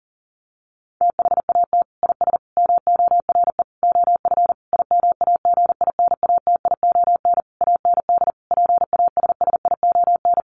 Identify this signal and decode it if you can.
Morse code, heard as a keyed tone, about 40 wpm, 700 Hz. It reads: THAT IS MORE OF IMAGINATION AND PASSION
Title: Sentences from Top 1300 Words 40wpm